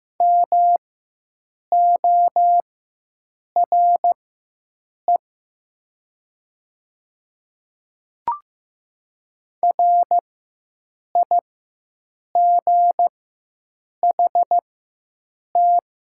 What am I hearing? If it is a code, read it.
Morse code, heard as a keyed tone, about 15 wpm, spaced as for 7 wpm, 700 Hz. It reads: MORE RIGHT
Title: 4x Character Spacing - Top 300 Words - 15wpm